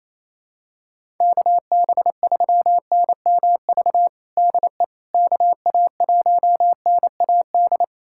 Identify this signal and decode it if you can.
Morse code, heard as a keyed tone, about 28 wpm, 700 Hz. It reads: KB3NMV DE KA1NAD
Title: USA Callsign DE Callsign - 6 Characters - 28wpm